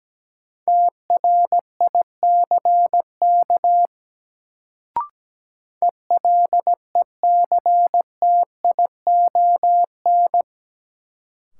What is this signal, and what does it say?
Morse code, heard as a keyed tone, about 17 wpm, 700 Hz. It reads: TRICK ELECTION
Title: Sentences from Top 1500 Words - New Word Drill - 17wpm